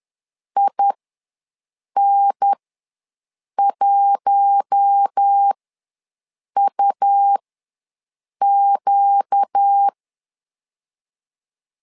Morse code, heard as a keyed tone, about 10 wpm, spaced as for 6 wpm, 800 Hz. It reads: IN1UQ
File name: LEZIONI / 4 Q1S ws 10wpm 2208